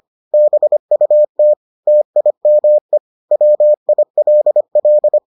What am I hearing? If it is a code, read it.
Morse code, heard as a keyed tone, about 25 wpm, 600 Hz. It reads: BUT TIME WILL